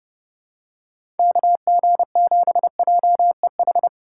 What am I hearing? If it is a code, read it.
Morse code, heard as a keyed tone, about 31 wpm, 700 Hz. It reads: KG7JEH
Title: USA Call Signs - 6 Characters - 4x Character Spacing - 30wpm